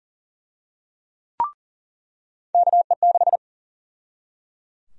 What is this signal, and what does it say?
Morse code, heard as a keyed tone, about 40 wpm, 700 Hz. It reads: KE6
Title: USA Call Sign Prefixes - 40wpm